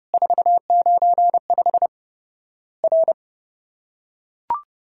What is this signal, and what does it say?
Morse code, heard as a keyed tone, about 30 wpm, 700 Hz. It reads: 495 R
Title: Worked All Europe DX Contest - QTC Exchange - 30wpm